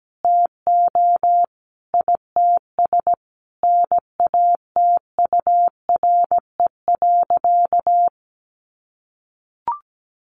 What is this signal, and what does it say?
Morse code, heard as a keyed tone, about 17 wpm, 700 Hz. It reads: TO ITS NATURE.
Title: Sentences from Top 700 Words 17wpm